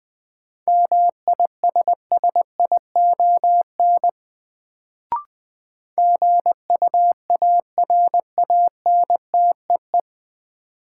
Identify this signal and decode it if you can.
Morse code, heard as a keyed tone, about 20 wpm, 700 Hz. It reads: MISSION GUARANTEE
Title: Sentences from Top 1600 Words - New Word Drill Alt - 20wpm